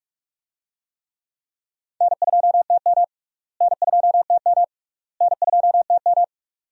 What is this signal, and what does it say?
Morse code, heard as a keyed tone, about 46 wpm, 700 Hz. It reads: N2TM N2TM N2TM
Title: USA Call Signs - 4 Characters - Repeated 3x - 45wpm